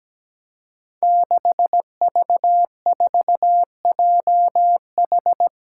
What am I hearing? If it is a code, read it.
Morse code, heard as a keyed tone, about 17 wpm, 700 Hz. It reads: 6V4JH
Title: Five Challenging Letter-Number Combinations 17wpm